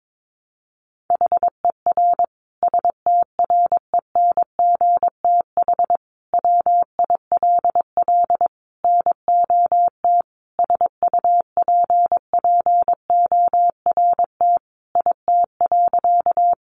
Morse code, heard as a keyed tone, about 22 wpm, 700 Hz. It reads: HER STRENGTH WILL NOT SUPPORT IT.
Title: Sentences from Top 1300 Words 22wpm